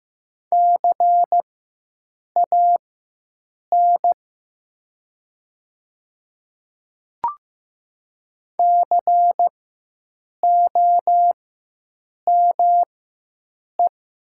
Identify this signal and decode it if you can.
Morse code, heard as a keyed tone, about 15 wpm, spaced as for 7 wpm, 700 Hz. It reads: CAN COME
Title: 4x Character Spacing - Top 100 Words - 15wpm